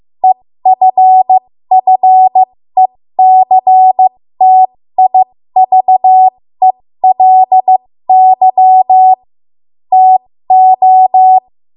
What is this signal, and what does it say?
Morse code, heard as a keyed tone, about 15 wpm, spaced as for 13 wpm, 750 Hz. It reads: EFFECTIVELY TO
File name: Morse / 13WPM